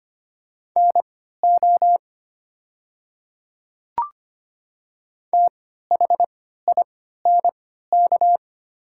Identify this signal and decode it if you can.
Morse code, heard as a keyed tone, about 25 wpm, spaced as for 14 wpm, 700 Hz. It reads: NO THINK
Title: 3x Character Spacing - Top 200 Words - 25wpm